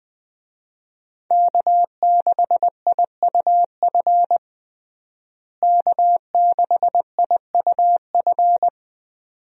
Morse code, heard as a keyed tone, about 21 wpm, 700 Hz. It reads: K6IUF K6IUF
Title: USA Call Signs - 5 Characters - Repeated 2x - 20wpm